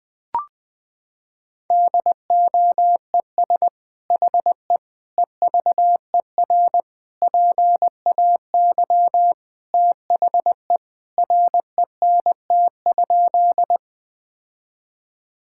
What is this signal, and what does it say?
Morse code, heard as a keyed tone, about 20 wpm, 700 Hz. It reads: DOES HE EVER PAY THE RENT?
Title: Sentences from Top 1000 Words - Review Part 5 - 20wpm